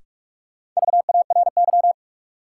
Morse code, heard as a keyed tone, about 46 wpm, 700 Hz. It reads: VARY
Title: Sentences from Top 1200 Words - New Word Drill Alt - 45wpm